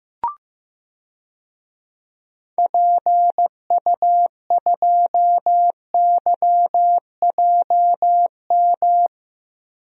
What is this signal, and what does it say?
Morse code, heard as a keyed tone, about 15 wpm, 700 Hz. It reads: PU2YJM
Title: DXCC-100 Call Signs - 6 Characters - 15wpm